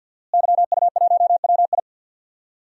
Morse code, heard as a keyed tone, about 49 wpm, 700 Hz. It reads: YU1WI
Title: DXCC-100 Call Signs - 5 Characters - Repeated 2x - 50wpm